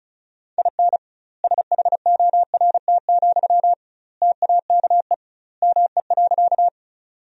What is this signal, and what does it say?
Morse code, heard as a keyed tone, about 35 wpm, 700 Hz. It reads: IN SHORT, TAKE ME.
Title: 2x Word-Spacing - Sentences from Top 400 Words - 35wpm